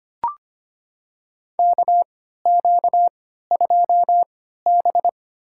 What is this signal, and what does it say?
Morse code, heard as a keyed tone, about 25 wpm, spaced as for 14 wpm, 700 Hz. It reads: KQ2B
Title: USA Call Signs - 4 Characters - 3x Character Spacing - 25wpm